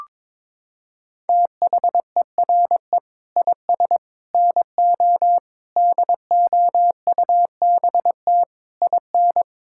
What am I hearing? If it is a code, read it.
Morse code, heard as a keyed tone, about 22 wpm, 700 Hz. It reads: THERE IS NO DOUBT IN
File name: Sets of 9 Words - 22wpm